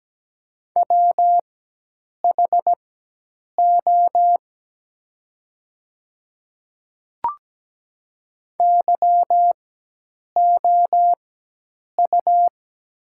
Morse code, heard as a keyed tone, about 17 wpm, spaced as for 8 wpm, 690 Hz. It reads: WHO YOU
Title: 4x Character Spacing - Top 100 Words - 17wpm